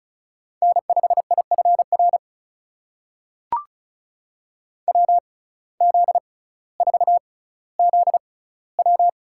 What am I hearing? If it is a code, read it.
Morse code, heard as a keyed tone, about 36 wpm, spaced as for 18 wpm, 700 Hz. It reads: N5IFR WZ4ZW
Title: USA Call Signs - 5 Characters - 6x Character Spacing - 35wpm